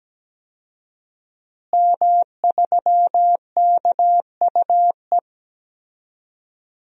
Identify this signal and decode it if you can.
Morse code, heard as a keyed tone, about 17 wpm, 700 Hz. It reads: M3KUE
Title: DXCC-100 Call Signs - 5 Characters - Repeated 2x - 17wpm